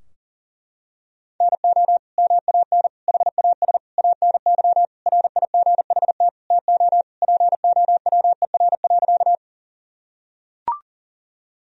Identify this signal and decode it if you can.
Morse code, heard as a keyed tone, about 40 wpm, 700 Hz. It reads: NO MAN HAS ANY RIGHT TO POWER.
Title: Sentences from Top 1000 Words - Review Part 2 - 40wpm